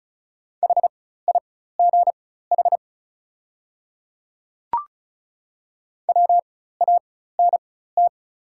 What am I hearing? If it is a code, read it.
Morse code, heard as a keyed tone, about 36 wpm, spaced as for 16 wpm, 690 Hz. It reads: HIGH WANT
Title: 4x Character Spacing - Top 200 Words - 35wpm